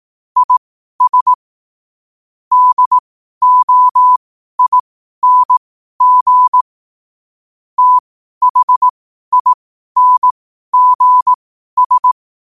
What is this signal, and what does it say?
Morse code, heard as a keyed tone, about 18 wpm, spaced as for 12 wpm, 1000 Hz. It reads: IS DOING THINGS
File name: 2024-09-14-12wpm